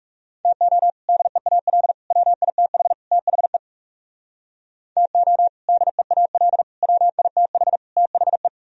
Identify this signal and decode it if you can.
Morse code, heard as a keyed tone, about 38 wpm, 700 Hz. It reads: TO DEAL WITH THE TO DEAL WITH THE